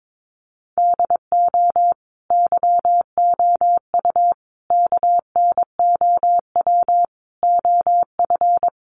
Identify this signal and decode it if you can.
Morse code, heard as a keyed tone, about 22 wpm, 700 Hz. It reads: DO YOU KNOW OF
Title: Sentences from Top 1200 Words 22wpm